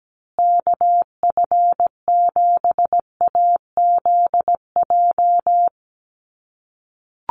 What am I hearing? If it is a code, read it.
Morse code, heard as a keyed tone, about 17 wpm, 700 Hz. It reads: KF7AZJ
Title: USA Call Signs Repeated 3x 17wpm